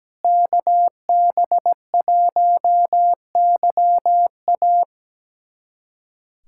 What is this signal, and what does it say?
Morse code, heard as a keyed tone, about 17 wpm, 700 Hz. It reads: KB1YA
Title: USA Callsign DE Callsign 17wpm